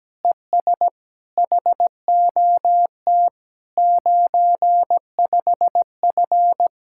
Morse code, heard as a keyed tone, about 17 wpm, 700 Hz. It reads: ES HOT 95F
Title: Speed-Racing Common QSO Phrases - 17wpm